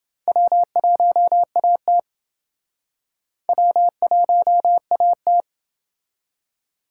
Morse code, heard as a keyed tone, about 29 wpm, 700 Hz. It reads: W1AT W1AT